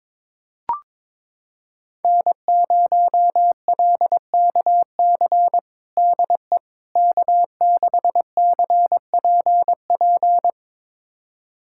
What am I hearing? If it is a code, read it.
Morse code, heard as a keyed tone, about 22 wpm, 700 Hz. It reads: N0LKC DE K6CPP
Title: USA Callsign DE Callsign - 5 Characters - 22wpm